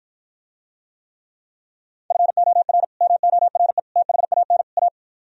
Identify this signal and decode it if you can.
Morse code, heard as a keyed tone, about 54 wpm, 700 Hz. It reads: FOR MORE THAN A